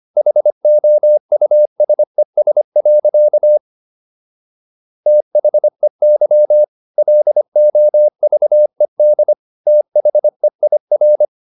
Morse code, heard as a keyed tone, about 25 wpm, 600 Hz. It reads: HOUSES. THEY LOVED THEIR